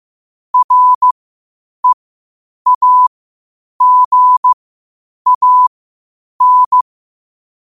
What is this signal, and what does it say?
Morse code, heard as a keyed tone, about 15 wpm, spaced as for 8 wpm, 1000 Hz. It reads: REAGAN
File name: qotdfiles / 8wpm